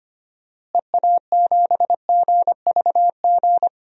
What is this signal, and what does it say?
Morse code, heard as a keyed tone, about 25 wpm, 700 Hz. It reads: EA7GVG
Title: DXCC-100 Call Signs - 6 Characters - Repeated 3x - 25wpm